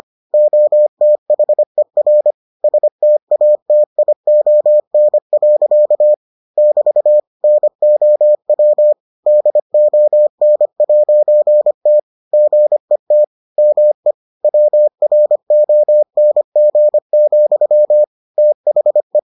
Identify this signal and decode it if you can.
Morse code, heard as a keyed tone, about 25 wpm, 600 Hz. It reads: OTHER STATION. <BT> NOW DON'T GET ME WRONG, THE